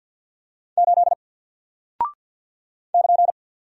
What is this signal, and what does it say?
Morse code, heard as a keyed tone, about 49 wpm, 700 Hz. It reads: 8 <KN>
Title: Single Character (Letter:Number:Punctuation:Prosign) Alt 50wpm